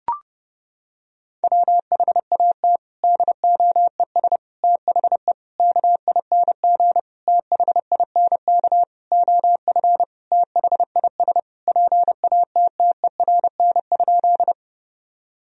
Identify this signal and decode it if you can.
Morse code, heard as a keyed tone, about 30 wpm, 700 Hz. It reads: WHAT DOES THE KING THINK OF THIS PATTERN?
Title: Sentences from Top 1000 Words - Review Part 2 - 30wpm